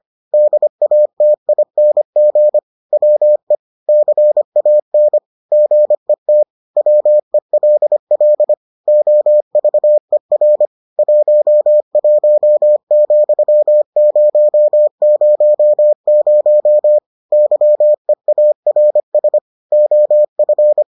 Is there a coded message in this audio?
Yes: Morse code from 0.3 to 20.8 s, heard as a keyed tone, about 25 wpm, 600 Hz. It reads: DATING WE CAN GET WELL OVER 11,000 YEARS OF